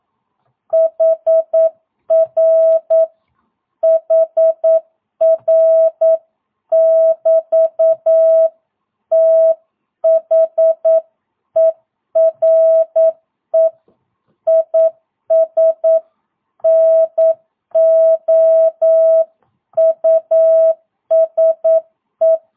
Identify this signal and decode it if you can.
Morse code, heard as a keyed tone, about 9 wpm, 660 Hz. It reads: HR HR <BT> THERE IS NO USE
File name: jh8cbh / 2022cw